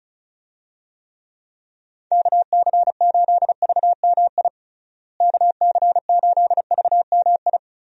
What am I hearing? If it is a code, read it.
Morse code, heard as a keyed tone, about 36 wpm, 700 Hz. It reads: KC8VMI KC8VMI
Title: USA Call Signs - 6 Characters - Repeated 3x - 35wpm